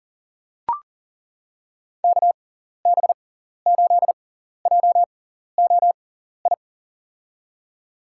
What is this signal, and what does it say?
Morse code, heard as a keyed tone, about 40 wpm, spaced as for 14 wpm, 700 Hz. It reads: KB8JOI
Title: USA Call Signs - 6x Character Spacing - 40wpm